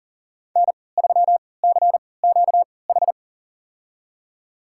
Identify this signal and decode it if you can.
Morse code, heard as a keyed tone, about 40 wpm, spaced as for 23 wpm, 700 Hz. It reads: N3CQH
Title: USA Call Signs - 5 Characters - 3x Character Spacing - 40wpm